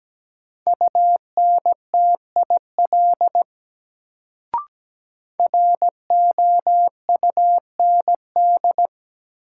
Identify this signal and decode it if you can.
Morse code, heard as a keyed tone, about 17 wpm, 700 Hz. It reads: UNTIL ROUND
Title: Sets of 5 Words Encore - New Word Drill - 17wpm